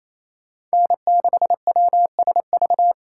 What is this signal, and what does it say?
Morse code, heard as a keyed tone, about 36 wpm, 700 Hz. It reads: N6WSV